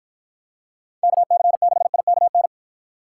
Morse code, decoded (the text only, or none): KC6IKN